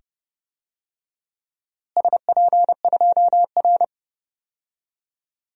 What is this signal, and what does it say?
Morse code, heard as a keyed tone, about 30 wpm, 700 Hz. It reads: SP2R
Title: DXCC-100 Call Signs - 4 Characters - Repeated 2x - 30wpm